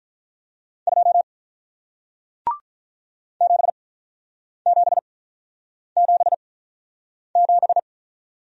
Morse code, heard as a keyed tone, about 48 wpm, spaced as for 9 wpm, 700 Hz. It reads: 2 7777